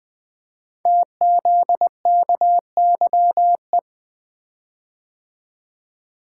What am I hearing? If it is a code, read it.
Morse code, heard as a keyed tone, about 20 wpm, 700 Hz. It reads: TZKYE
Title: Code Groups - 5 Letters - 20wpm